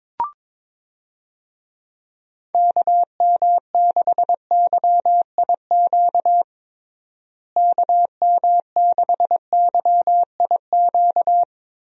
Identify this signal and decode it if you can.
Morse code, heard as a keyed tone, about 22 wpm, 700 Hz. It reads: KM6YIQ KM6YIQ
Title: USA Call Signs - 6 Characters - Repeated 2x - 22wpm